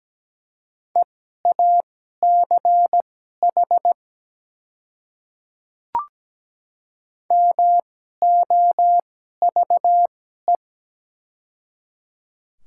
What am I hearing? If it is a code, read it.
Morse code, heard as a keyed tone, about 17 wpm, spaced as for 12 wpm, 700 Hz. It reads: EACH MOVE
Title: 2x Character Spacing - Top 200 Words - 17wpm